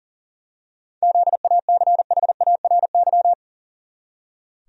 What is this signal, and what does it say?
Morse code, heard as a keyed tone, about 37 wpm, 700 Hz. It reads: ZACHARY
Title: USA Common Names - 40wpm